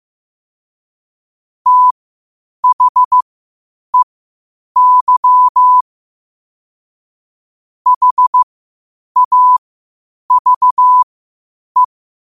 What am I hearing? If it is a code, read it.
Morse code, heard as a keyed tone, about 15 wpm, spaced as for 8 wpm, 1000 Hz. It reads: THEY HAVE
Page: Quotes for Fri, 15 Aug 2025 in Morse Code at 8 words per minute.